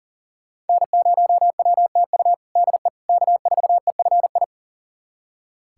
Morse code, heard as a keyed tone, about 37 wpm, 700 Hz. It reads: N0WTU DE K4EFI
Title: USA Callsign DE Callsign - 5 Characters - 40wpm